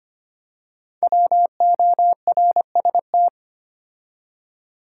Morse code, heard as a keyed tone, about 26 wpm, 700 Hz. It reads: WORST
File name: 5 Letter Words - First 3 Letters Given - 25wpm